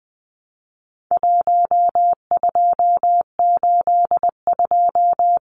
Morse code, heard as a keyed tone, about 20 wpm, 700 Hz. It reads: 1282
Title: Numbers - 4-Digit 20wpm